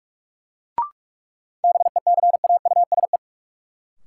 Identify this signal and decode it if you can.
Morse code, heard as a keyed tone, about 46 wpm, 700 Hz. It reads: BECAUSE
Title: Sets of 2 Words - New Word Drill - 45wpm